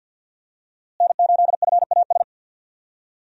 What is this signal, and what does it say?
Morse code, heard as a keyed tone, about 53 wpm, 700 Hz. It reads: N8FAS
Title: USA Call Signs - 5 Characters - Repeated 2x - 50wpm